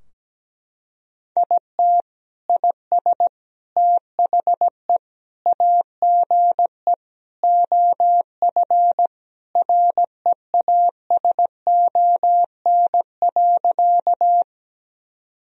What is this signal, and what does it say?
Morse code, heard as a keyed tone, about 17 wpm, 700 Hz. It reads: IT IS THE AGE OF REASON.
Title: Sentences from Top 1000 Words - Review Part 3 - 17wpm